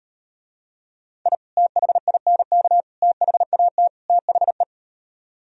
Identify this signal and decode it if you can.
Morse code, heard as a keyed tone, about 39 wpm, 700 Hz. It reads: I THINK THAT THE